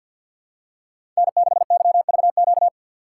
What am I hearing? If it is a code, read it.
Morse code, heard as a keyed tone, about 52 wpm, 700 Hz. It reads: N6YVQ
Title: USA Call Signs - 5 Characters - 6x Character Spacing - 50wpm